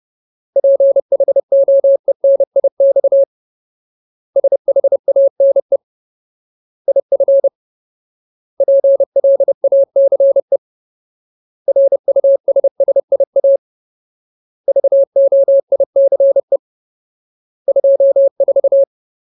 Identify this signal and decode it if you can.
Morse code, heard as a keyed tone, about 30 wpm, 550 Hz. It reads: PHOENIX SHANE IF PLACE RUSSIA VOICE 24
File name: Ultimate Word List / 30WPM Ultimate